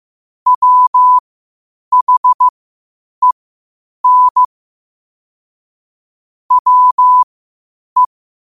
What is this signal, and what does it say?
Morse code, heard as a keyed tone, about 15 wpm, spaced as for 8 wpm, 1000 Hz. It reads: WHEN WE